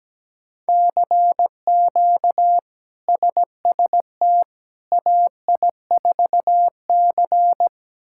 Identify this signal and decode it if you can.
Morse code, heard as a keyed tone, about 17 wpm, 700 Hz. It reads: CQ SST AI4C
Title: K1USN SST QSO Practice - 17wpm